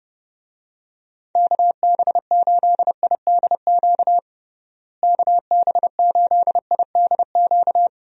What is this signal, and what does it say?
Morse code, heard as a keyed tone, about 32 wpm, 700 Hz. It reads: KB8IDQ KB8IDQ
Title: USA Call Signs - 6 Characters - Repeated 3x - 30wpm